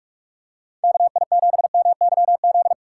Morse code, heard as a keyed tone, about 46 wpm, 700 Hz. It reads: KI7MYZ
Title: USA Call Signs - 3x Character Spacing - 45wpm